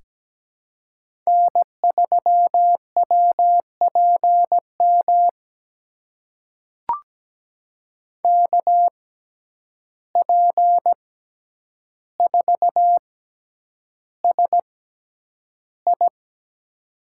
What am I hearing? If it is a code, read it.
Morse code, heard as a keyed tone, about 17 wpm, spaced as for 9 wpm, 700 Hz. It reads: N3WPM KP4SI
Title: USA Call Signs - 5 Characters - 6x Character Spacing - 17wpm